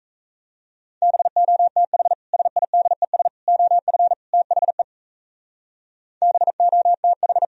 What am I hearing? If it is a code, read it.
Morse code, heard as a keyed tone, about 35 wpm, 700 Hz. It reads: BOTH SIDES OF THE BOTH
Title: Speed-Racing - Sets of 4 Words 28wpm